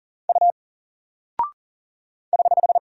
Morse code, heard as a keyed tone, about 40 wpm, 700 Hz. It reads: U <HH>
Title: Single Character (Letter:Number:Punctuation:Prosign) Alt 40wpm